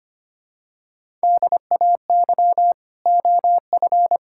Morse code, heard as a keyed tone, about 25 wpm, 700 Hz. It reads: DAY OF
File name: practice / 25wpm / Sentences with 9-Word Phrases - 25wpm